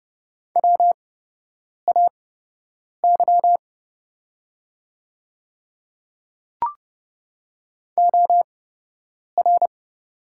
Morse code, heard as a keyed tone, about 30 wpm, spaced as for 8 wpm, 700 Hz. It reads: WAY OR
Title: 8x Character Spacing - Top 200 Words - 30wpm